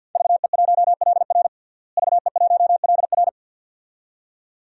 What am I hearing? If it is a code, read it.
Morse code, heard as a keyed tone, about 51 wpm, 700 Hz. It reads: VE1LR VE1LR
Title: DXCC-100 Call Signs - 5 Characters - Repeated 2x - 50wpm